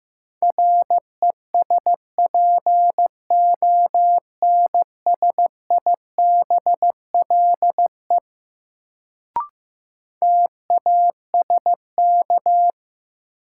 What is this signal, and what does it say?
Morse code, heard as a keyed tone, about 15 wpm, 700 Hz. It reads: RESPONSIBLE TASK
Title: Sentences from Top 1000 Words - Review Part 7 - New Word Drill - 15wpm